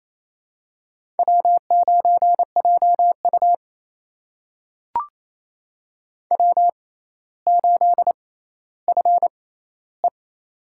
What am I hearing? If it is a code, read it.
Morse code, heard as a keyed tone, about 28 wpm, spaced as for 14 wpm, 700 Hz. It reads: W9JU W8FE
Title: USA Call Signs - 6x Character Spacing - 28wpm